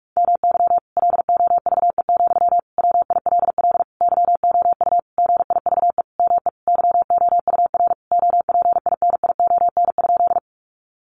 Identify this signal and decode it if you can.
Morse code, heard as a keyed tone, about 44 wpm, 700 Hz. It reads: MY LOVE, WILL YOU GIVE ME YOUR OPINION?
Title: Sentences from Top 1200 Words 45wpm